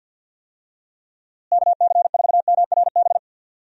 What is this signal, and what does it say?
Morse code, heard as a keyed tone, about 52 wpm, 700 Hz. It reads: KC4MRB
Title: USA Call Signs - 6 Characters - Repeated 2x - 50wpm